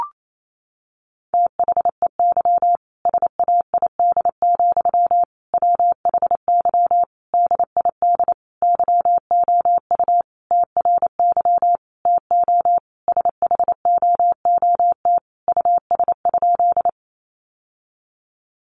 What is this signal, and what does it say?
Morse code, heard as a keyed tone, about 28 wpm, 700 Hz. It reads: THEY SAID, WHY DID YOU TRY TO SHOOT US?
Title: Sentences from Top 1200 Words 28wpm